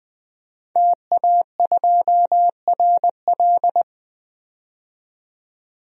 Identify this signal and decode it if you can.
Morse code, heard as a keyed tone, about 20 wpm, 700 Hz. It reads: TA2RL
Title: DXCC-100 Call Signs - 5 Characters - 20wpm